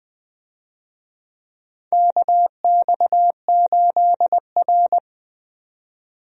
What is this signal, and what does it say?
Morse code, heard as a keyed tone, about 20 wpm, 700 Hz. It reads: KX8R
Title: USA Call Signs - 4 Characters - 20wpm